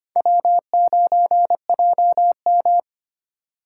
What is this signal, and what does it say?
Morse code, heard as a keyed tone, about 31 wpm, 700 Hz. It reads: W9JM